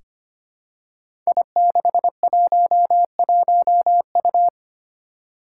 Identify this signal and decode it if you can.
Morse code, heard as a keyed tone, about 25 wpm, 700 Hz. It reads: I611U
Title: Five Challenging Letter-Number Combinations 25wpm